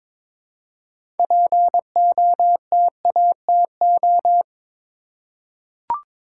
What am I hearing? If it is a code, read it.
Morse code, heard as a keyed tone, about 22 wpm, 700 Hz. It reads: POTATO
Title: Six Letter Words - 22wpm